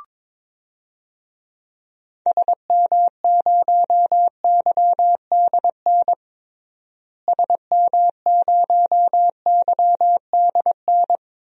Morse code, heard as a keyed tone, about 22 wpm, 700 Hz. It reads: SM0YDN SM0YDN
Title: DXCC-100 Call Signs - 6 Characters - Repeated 2x - 22wpm